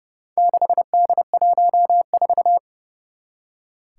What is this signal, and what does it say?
Morse code, heard as a keyed tone, about 30 wpm, 700 Hz. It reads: 6D14
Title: Four Challenging Letter-Number Combinations 30wpm